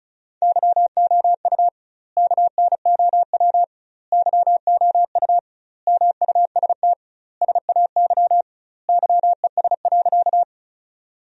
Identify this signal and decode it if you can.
Morse code, heard as a keyed tone, about 36 wpm, 700 Hz. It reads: YOU KNOW YOU MUST SAY YES.
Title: 2x Word-Spacing - Sentences from Top 500 Words - 35wpm